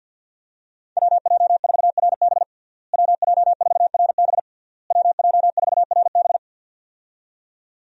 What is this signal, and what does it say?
Morse code, heard as a keyed tone, about 47 wpm, 700 Hz. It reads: WJ4RB WJ4RB WJ4RB
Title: USA Call Signs - 5 Characters - Repeated 3x - 50wpm